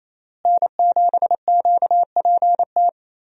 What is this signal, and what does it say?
Morse code, heard as a keyed tone, about 28 wpm, 700 Hz. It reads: N7QPT
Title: USA Call Signs - 5 Characters - Repeated 3x - 28wpm